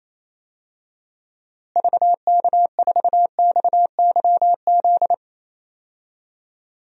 Morse code, heard as a keyed tone, about 28 wpm, 700 Hz. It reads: VK4XYZ
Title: DXCC-100 Call Signs - 6 Characters - Repeated 3x - 28wpm